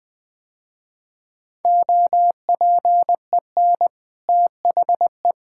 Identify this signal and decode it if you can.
Morse code, heard as a keyed tone, about 20 wpm, 700 Hz. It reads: OPEN THE
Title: Easy Sentences with Hints - 20wpm